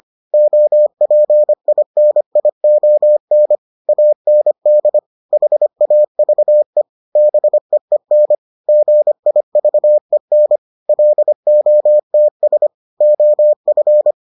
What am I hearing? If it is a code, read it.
Morse code, heard as a keyed tone, about 25 wpm, 600 Hz. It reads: OPINION AND HAVE BEEN GIVEN LOTS OF